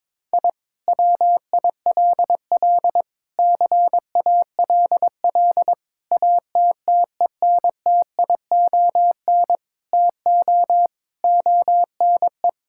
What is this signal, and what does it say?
Morse code, heard as a keyed tone, about 22 wpm, 700 Hz. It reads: I WILL CALL ATTENTION TO ONE
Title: Sentences from Top 1000 Words - Review Part 6 - 22wpm